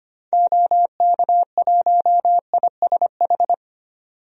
Morse code, heard as a keyed tone, about 25 wpm, 700 Hz. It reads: OK1ISH
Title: DXCC-100 Call Signs - 6 Characters - 25wpm